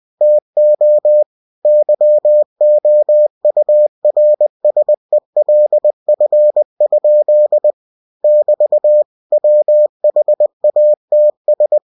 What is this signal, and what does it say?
Morse code, heard as a keyed tone, about 20 wpm, 600 Hz. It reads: TO YOURSELF? <BT> WHATS
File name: cw-books / winnie-the-pooh / winnie-20wpm-page-0110